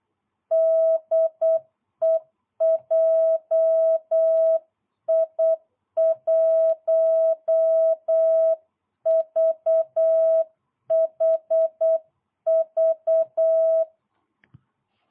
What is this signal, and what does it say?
Morse code, heard as a keyed tone, about 8 wpm, 660 Hz. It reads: DE JI1VHV